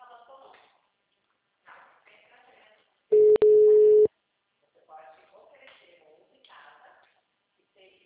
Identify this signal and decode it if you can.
Ringback tone (Europe, 425 Hz), with speech.